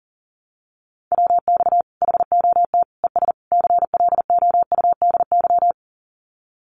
Morse code, heard as a keyed tone, about 40 wpm, 700 Hz. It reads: WX HOT ES CLOUDY
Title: Common QSO Phrases 40wpm